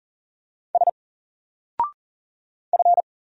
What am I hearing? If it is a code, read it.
Morse code, heard as a keyed tone, about 38 wpm, 700 Hz. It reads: S F
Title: Single Character (Letter:Number:Punctuation:Prosign) 40wpm